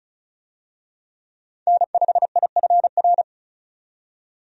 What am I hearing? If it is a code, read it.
Morse code, heard as a keyed tone, about 34 wpm, 700 Hz. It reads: N5IFR
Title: USA Call Signs - 5 Characters - 35wpm